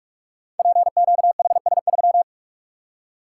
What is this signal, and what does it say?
Morse code, heard as a keyed tone, about 44 wpm, 690 Hz. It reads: PQHS3
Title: Code Groups - 5 Letter:Numbers - 45wpm